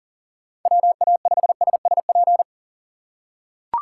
Morse code, heard as a keyed tone, about 40 wpm, 700 Hz. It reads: WA5SSP
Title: USA Call Signs - 6 Characters - 2x Character Spacing - 40wpm